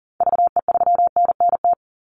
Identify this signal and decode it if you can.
Morse code, heard as a keyed tone, about 40 wpm, 700 Hz. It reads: VE3NNT
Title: Call Signs - 40wpm